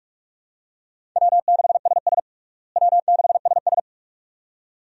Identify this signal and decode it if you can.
Morse code, heard as a keyed tone, about 46 wpm, 700 Hz. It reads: W6SS W6SS
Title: USA Call Signs - 4 Characters - Repeated 2x - 45wpm